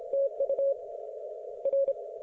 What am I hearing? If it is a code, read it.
Morse code, heard as a keyed tone, about 27 wpm, 550 Hz. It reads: TU R